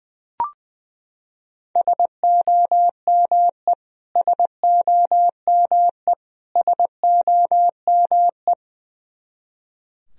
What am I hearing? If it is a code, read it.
Morse code, heard as a keyed tone, about 21 wpm, 700 Hz. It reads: SOME SOME SOME
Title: Top 100 Words Repeated 20wpm